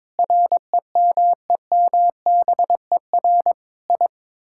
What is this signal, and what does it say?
Morse code, heard as a keyed tone, about 22 wpm, 700 Hz. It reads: REMEMBER I
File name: practice / 22wpm / Sets of 8 Words - 22wpm